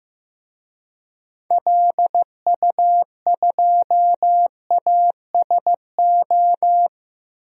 Morse code, heard as a keyed tone, about 15 wpm, 700 Hz. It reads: LU2ASO
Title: DXCC-100 Call Signs - 6 Characters - Repeated 3x - 15wpm